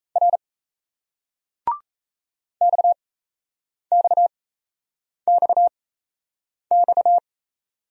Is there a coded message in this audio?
Yes: Morse code from 0.2 to 7.2 s, heard as a keyed tone, about 38 wpm, spaced as for 8 wpm, 700 Hz. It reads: R XXXX